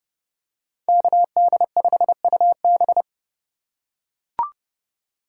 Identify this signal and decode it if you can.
Morse code, heard as a keyed tone, about 27 wpm, 700 Hz. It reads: KD5UB